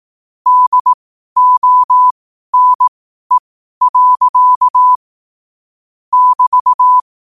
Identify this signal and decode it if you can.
Morse code, heard as a keyed tone, about 18 wpm, spaced as for 12 wpm, 1000 Hz. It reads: DONE. <BT>
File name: qotdfiles / 12wpm